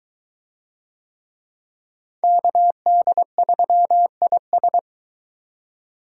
Morse code, heard as a keyed tone, about 21 wpm, 700 Hz. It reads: KD3IS